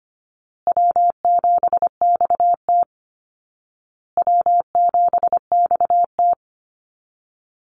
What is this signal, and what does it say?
Morse code, heard as a keyed tone, about 25 wpm, 700 Hz. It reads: W7XT W7XT
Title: Call Signs Repeated 2x - 25wpm